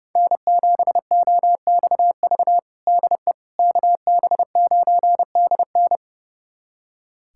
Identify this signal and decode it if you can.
Morse code, heard as a keyed tone, about 29 wpm, 700 Hz. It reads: N7OXV DE KB9DN
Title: USA Callsign DE Callsign - 5 Characters - 30wpm